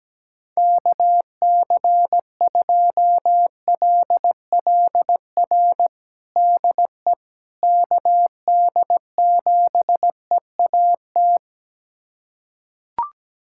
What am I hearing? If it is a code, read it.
Morse code, heard as a keyed tone, about 17 wpm, 700 Hz. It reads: KC2LLR DE KD7EAT
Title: USA Callsign DE Callsign - 6 Characters - 17wpm